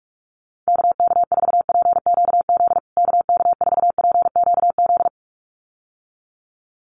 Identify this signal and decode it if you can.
Morse code, heard as a keyed tone, about 44 wpm, 700 Hz. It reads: KK4PQZ KK4PQZ
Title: USA Call Signs Repeated 2x 45wpm